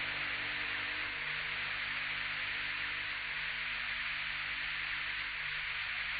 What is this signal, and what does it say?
Emergency Alert System (SAME) tones (header bursts), heard as harsh buzzing bursts. It is heard from 0.8 to 5.8 s.